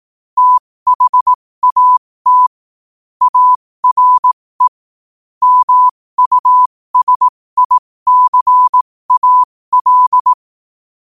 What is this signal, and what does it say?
Morse code, heard as a keyed tone, about 18 wpm, spaced as for 15 wpm, 1000 Hz. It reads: THAT ARE MUSICAL
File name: qotdfiles / 15wpm